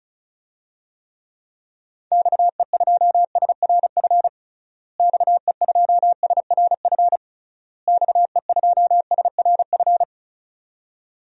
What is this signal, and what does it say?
Morse code, heard as a keyed tone, about 34 wpm, 700 Hz. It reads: XE2SRF XE2SRF XE2SRF
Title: DXCC-100 Call Signs - 6 Characters - Repeated 3x - 35wpm